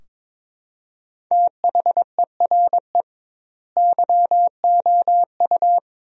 Morse code, heard as a keyed tone, about 22 wpm, 700 Hz. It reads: THERE YOU
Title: 2x Word-Spacing - Sentences from Top 100 Words - 22wpm